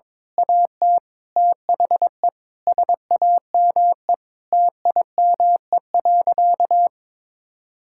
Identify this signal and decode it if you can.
Morse code, heard as a keyed tone, about 21 wpm, 700 Hz. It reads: AT THE SAME TIME.
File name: Difficult Sentences with Hints - 22wpm